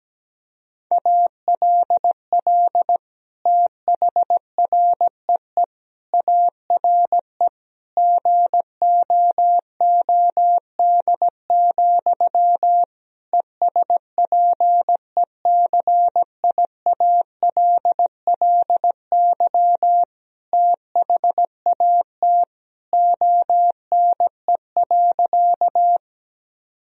Morse code, heard as a keyed tone, about 17 wpm, 700 Hz. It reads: ALL THREE ARE GOOD, ESPECIALLY THAT ONE.
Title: Sentences from Top 1000 Words - Review Part 4 - 17wpm